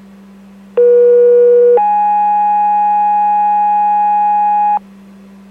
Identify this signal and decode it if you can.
Two-tone page, heard as a one-second tone, then a three-second tone, about 480 then 830 Hz, with noise. It is heard from 0.8 to 4.8 s.